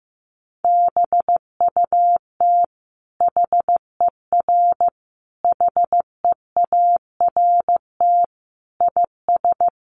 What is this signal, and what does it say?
Morse code, heard as a keyed tone, about 14 wpm, 700 Hz. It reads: BUT HER HEART IS
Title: Sentences from Top 1200 Words 15wpm